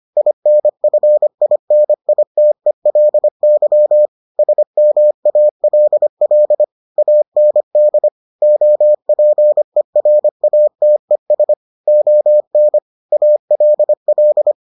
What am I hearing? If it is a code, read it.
Morse code, heard as a keyed tone, about 25 wpm, 600 Hz. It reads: INFINITELY SMALL AND OPERATES ON ALL